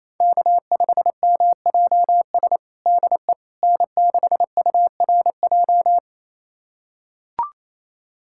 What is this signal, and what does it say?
Morse code, heard as a keyed tone, about 27 wpm, 700 Hz. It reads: K5MJS DE N6URJ
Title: USA Callsign DE Callsign - 5 Characters - 28wpm